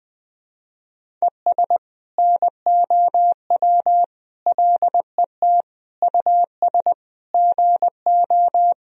Morse code, heard as a keyed tone, about 20 wpm, 700 Hz. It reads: ES NOW LET US GO
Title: Sentences from Top 300 Words and Abbreviations 20wpm